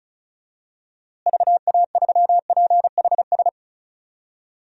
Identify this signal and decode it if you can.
Morse code, heard as a keyed tone, about 34 wpm, 700 Hz. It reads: VA3PHS
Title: DXCC-100 Call Signs - 6 Characters - Repeated 3x - 35wpm